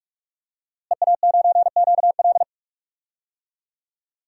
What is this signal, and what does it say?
Morse code, heard as a keyed tone, about 46 wpm, 700 Hz. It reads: EA9QL
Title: DXCC-100 Call Signs - 5 Characters - Repeated 3x - 45wpm